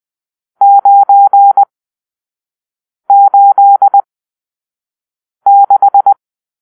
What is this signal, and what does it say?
Morse code, heard as a keyed tone, about 20 wpm, spaced as for 5 wpm, 800 Hz. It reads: 986
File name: arrl / Grp-7 Drill-3